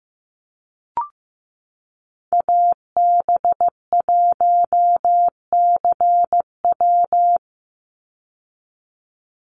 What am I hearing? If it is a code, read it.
Morse code, heard as a keyed tone, about 15 wpm, 700 Hz. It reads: AB1CW
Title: Call Signs Repeated 3x - 15wpm